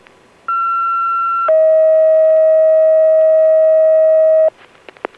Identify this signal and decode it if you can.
Two-tone page, heard as a one-second tone, then a three-second tone, about 1360 then 640 Hz. It is heard from 0.5 to 4.5 s.